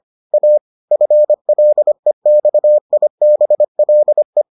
Morse code, heard as a keyed tone, about 25 wpm, 600 Hz. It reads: A FLEXIBLE